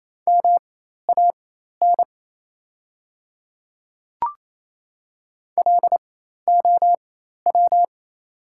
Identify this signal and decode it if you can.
Morse code, heard as a keyed tone, about 28 wpm, spaced as for 13 wpm, 690 Hz. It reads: MAN LOW
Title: 4x Character Spacing - Top 200 Words - 28wpm